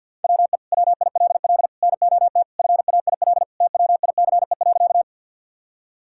Morse code, heard as a keyed tone, about 50 wpm, 700 Hz. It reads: WE WILL NOT FAIL TWICE.
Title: Sentences from Top 1600 Words 50wpm